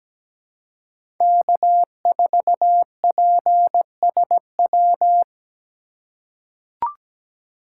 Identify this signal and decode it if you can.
Morse code, heard as a keyed tone, about 17 wpm, 700 Hz. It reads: K4PSW
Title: USA Call Signs - 2x Character Spacing - 17wpm